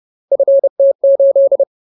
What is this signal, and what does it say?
Morse code, heard as a keyed tone, about 30 wpm, 550 Hz. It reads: FT8
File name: Ultimate Word List / 30WPM Ultimate